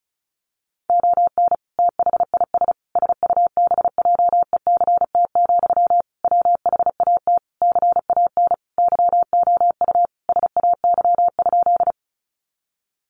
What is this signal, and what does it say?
Morse code, heard as a keyed tone, about 34 wpm, 700 Hz. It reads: ON THIS SUBJECT, WHAT CAN YOU SAY?
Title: Sentences from Top 800 Words 35wpm